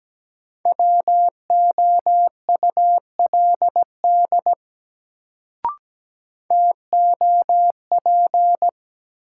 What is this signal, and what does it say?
Morse code, heard as a keyed tone, about 17 wpm, 700 Hz. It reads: WOULD TOP
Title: Sets of 2 Words - New Word Drill - 17wpm